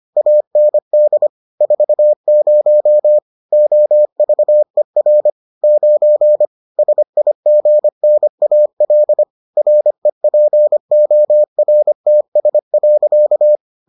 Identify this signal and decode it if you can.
Morse code, heard as a keyed tone, about 25 wpm, 600 Hz. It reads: AND 40 OVER 9 SIGNAL REPORTS.